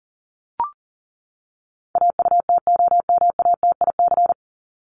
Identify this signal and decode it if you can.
Morse code, heard as a keyed tone, about 40 wpm, 700 Hz. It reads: AUTOMATIC
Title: Top 1751-2000 Words 40wpm